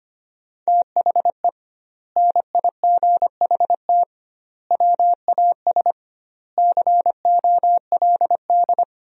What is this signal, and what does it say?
Morse code, heard as a keyed tone, about 25 wpm, 700 Hz. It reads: THE NIGHT WAS COLD
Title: 2x Word-Spacing - Sentences from Top 400 Words - 25wpm